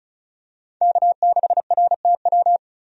Morse code, heard as a keyed tone, about 36 wpm, 700 Hz. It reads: K6RTW
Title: USA Call Signs - 5 Characters - Repeated 2x - 35wpm